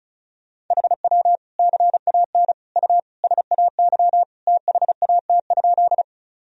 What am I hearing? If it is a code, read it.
Morse code, heard as a keyed tone, about 34 wpm, 690 Hz. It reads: HW CAN U SAY THAT?
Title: Sentences from Top 200 Words and Abbreviations 35wpm